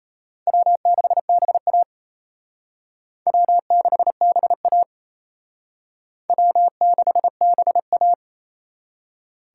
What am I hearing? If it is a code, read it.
Morse code, heard as a keyed tone, about 32 wpm, 700 Hz. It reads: W6BA W6BA W6BA
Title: Speed Racing - USA Call Signs 28wpm